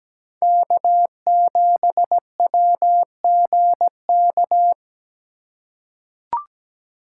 Morse code, heard as a keyed tone, about 17 wpm, 700 Hz. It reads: K7WGK
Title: USA Call Signs - 5 Characters - 3x Character Spacing - 17wpm